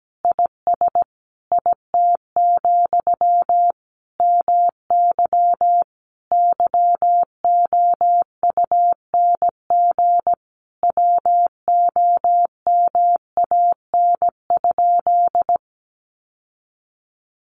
Morse code, heard as a keyed tone, about 17 wpm, 700 Hz. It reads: IS IT, MY YOUNG WOMAN?
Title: Sentences from Top 1100 Words 17wpm